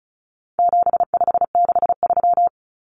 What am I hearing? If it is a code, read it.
Morse code, heard as a keyed tone, about 34 wpm, 700 Hz. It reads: 7563
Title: Numbers - 4-Digit 35wpm